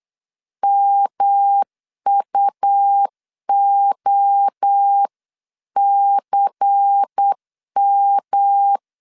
Morse code, heard as a keyed tone, about 8 wpm, 800 Hz. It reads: MUOCM
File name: CW / DET / 1 6 1500C S B 8wpm 5142